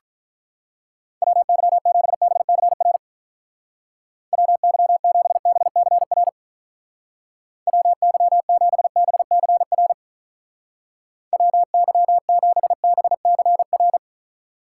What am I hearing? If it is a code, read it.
Morse code, heard as a keyed tone, about 44 wpm, 700 Hz. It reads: WY7BCR WY7BCR WY7BCR WY7BCR